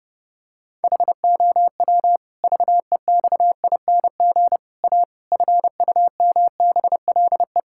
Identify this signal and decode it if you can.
Morse code, heard as a keyed tone, about 30 wpm, 700 Hz. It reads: HOW VEXING A FUMBLE
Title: Pangram Challenge v2 - 30wpm